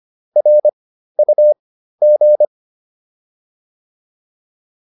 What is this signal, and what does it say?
Morse code, heard as a keyed tone, about 25 wpm, spaced as for 13 wpm, 600 Hz. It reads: RUG
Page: intermediate-practice-files